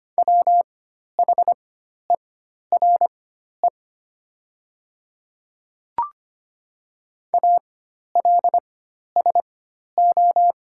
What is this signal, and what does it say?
Morse code, heard as a keyed tone, about 25 wpm, spaced as for 12 wpm, 700 Hz. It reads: WHERE ALSO
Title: 4x Character Spacing - Top 200 Words - 25wpm